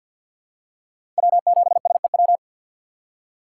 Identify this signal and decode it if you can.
Morse code, heard as a keyed tone, about 49 wpm, 700 Hz. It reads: W7SEW
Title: USA Call Signs - 5 Characters - 2x Character Spacing - 50wpm